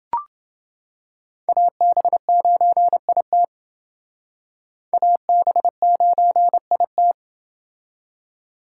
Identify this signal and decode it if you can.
Morse code, heard as a keyed tone, about 28 wpm, 700 Hz. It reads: AB9IT AB9IT